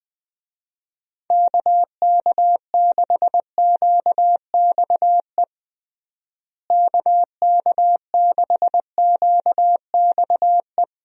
Morse code, heard as a keyed tone, about 20 wpm, 700 Hz. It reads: KK6QXE KK6QXE
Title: USA Call Signs - 6 Characters - Repeated 2x - 20wpm